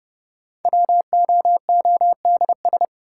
Morse code, heard as a keyed tone, about 30 wpm, 700 Hz. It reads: WOODS
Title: Sentences from Top 1000 Words - Review Part 2 - New Word Drill - 30wpm